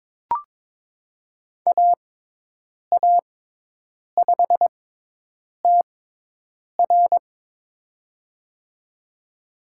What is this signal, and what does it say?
Morse code, heard as a keyed tone, about 22 wpm, spaced as for 8 wpm, 700 Hz. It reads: AA5TR
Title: USA Call Signs - 5 Characters - 6x Character Spacing - 22wpm